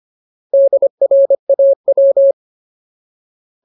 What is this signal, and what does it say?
Morse code, heard as a keyed tone, about 25 wpm, 550 Hz. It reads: DRAW
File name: Ultimate Word List / 25WPM Ultimate